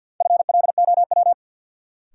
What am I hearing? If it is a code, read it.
Morse code, heard as a keyed tone, about 52 wpm, 700 Hz. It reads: FLOW
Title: Sentences from Top 1200 Words - New Word Drill - 50wpm